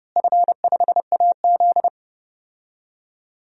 Morse code, heard as a keyed tone, about 32 wpm, 700 Hz. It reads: F5AZ
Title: DXCC-100 Call Signs - 4 Characters - Repeated 2x - 30wpm